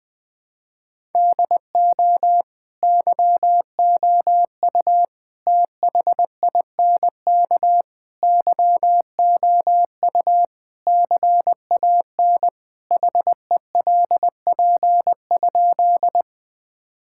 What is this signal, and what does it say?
Morse code, heard as a keyed tone, about 20 wpm, 690 Hz. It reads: DO YOU THINK YOU CAN HELP?
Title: Sentences with 4-Word Phrases - Spoken Beginning - 20wpm